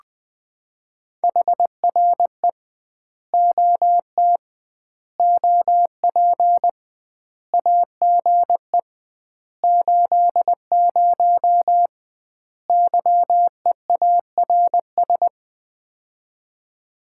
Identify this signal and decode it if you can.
Morse code, heard as a keyed tone, about 20 wpm, 700 Hz. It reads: HRE OT OP AGE 80 YEARS
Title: Common QSO Phrases - 2x Word-Spacing - 20wpm